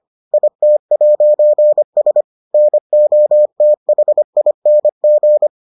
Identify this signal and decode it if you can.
Morse code, heard as a keyed tone, about 25 wpm, 600 Hz. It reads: IT'S NOTHING